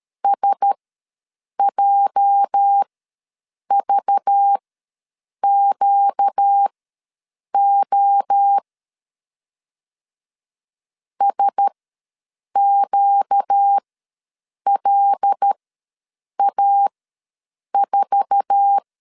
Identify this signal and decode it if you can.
Morse code, heard as a keyed tone, about 13 wpm, spaced as for 7 wpm, 800 Hz. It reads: SJVQO SQLA4
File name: LEZIONI / 8 YJ45 / 8 YJ4 ws 12wpm 3249